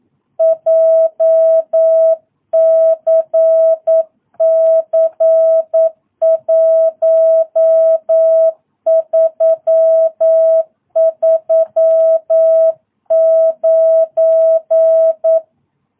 Morse code, heard as a keyed tone, about 9 wpm, 660 Hz. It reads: JCC 1339
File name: jh8cbh / 2022cw